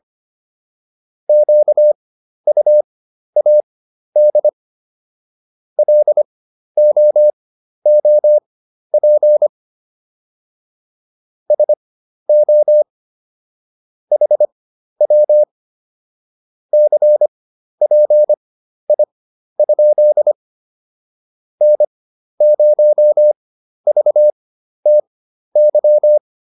Morse code, heard as a keyed tone, about 25 wpm, spaced as for 12 wpm, 600 Hz. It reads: QUAD LOOP SO HW CPI? N0VTY